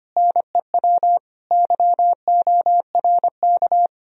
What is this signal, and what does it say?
Morse code, heard as a keyed tone, about 33 wpm, 700 Hz. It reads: NEW YORK